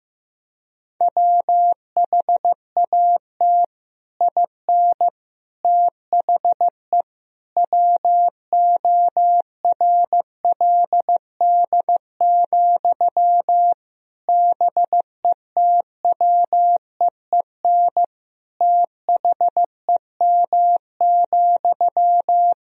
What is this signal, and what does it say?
Morse code, heard as a keyed tone, about 15 wpm, 700 Hz. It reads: WHAT IN THE WORLD, BETWEEN THEM,
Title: Sentences from Top 1000 Words - Review Part 1 - 15wpm